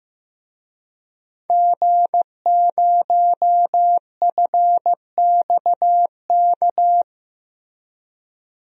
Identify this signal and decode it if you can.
Morse code, heard as a keyed tone, about 15 wpm, 700 Hz. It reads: G0FXK
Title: DXCC-100 Call Signs - 5 Characters - Repeated 2x - 15wpm